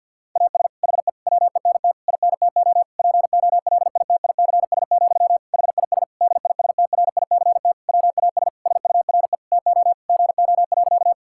Morse code, heard as a keyed tone, about 50 wpm, 700 Hz. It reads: AS HE WENT INTO POLITICS, HIS DISTRICT WAS SURE TO GO.
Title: Sentences from Top 1600 Words 50wpm